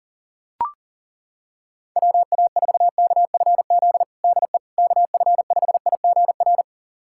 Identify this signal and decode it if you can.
Morse code, heard as a keyed tone, about 40 wpm, 700 Hz. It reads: WA4KFZ DE KF5IGR
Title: USA Callsign DE Callsign - 6 Characters - 40wpm